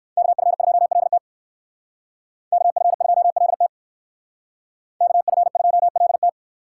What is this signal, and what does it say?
Morse code, heard as a keyed tone, about 58 wpm, 700 Hz. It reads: KF2LT KF2LT KF2LT